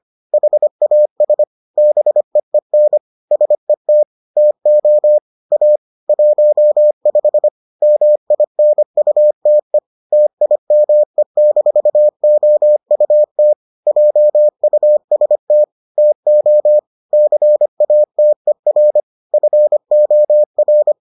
Morse code, heard as a keyed tone, about 25 wpm, 600 Hz. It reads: HAS BEEN SET TO A 15 MINUTE TIME-OUT JUST TO CATER FOR